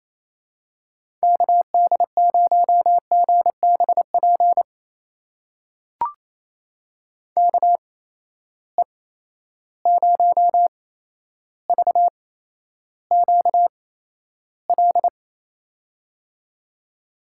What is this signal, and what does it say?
Morse code, heard as a keyed tone, about 27 wpm, spaced as for 12 wpm, 700 Hz. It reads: KD0GBP KE0VQL
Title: USA Call Signs - 6 Characters - 8x Character Spacing - 28wpm